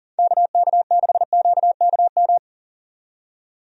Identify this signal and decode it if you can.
Morse code, heard as a keyed tone, about 40 wpm, 700 Hz. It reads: KK6QKM
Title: USA Call Signs - 6 Characters - 40wpm